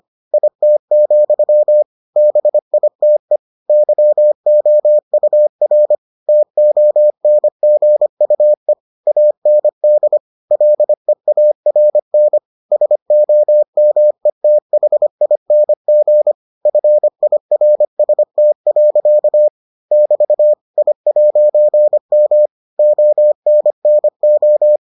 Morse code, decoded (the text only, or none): IT, BITE YOUR TONGUE AND LEARN SOMETHING FIRST. <BT> I'M ONNO